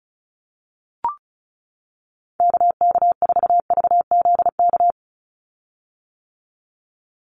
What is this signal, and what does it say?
Morse code, heard as a keyed tone, about 34 wpm, 700 Hz. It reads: KK4VZK
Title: USA Call Signs 35wpm